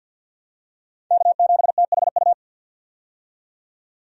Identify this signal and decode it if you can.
Morse code, heard as a keyed tone, about 50 wpm, 700 Hz. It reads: K7THU
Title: USA Call Signs - 5 Characters - 50wpm